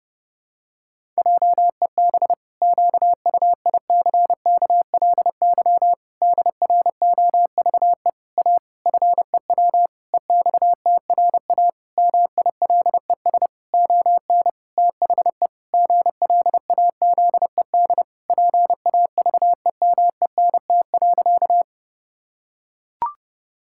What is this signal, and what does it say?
Morse code, heard as a keyed tone, about 32 wpm, 700 Hz. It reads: JEB QUICKLY DROVE A FEW EXTRA MILES ON THE GLAZED PAVEMENT.
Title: Pangram Challenge v2 - 30wpm